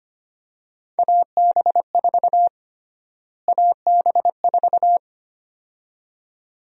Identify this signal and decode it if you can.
Morse code, heard as a keyed tone, about 25 wpm, 700 Hz. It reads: AB4 AB4
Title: USA Call Sign Prefixes - Repeated 2x - 25wpm